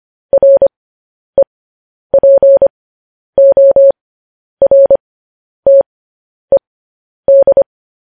Morse code, heard as a keyed tone, about 25 wpm, spaced as for 10 wpm, 550 Hz. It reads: REPORTED
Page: news headlines in morse code at 10 wpm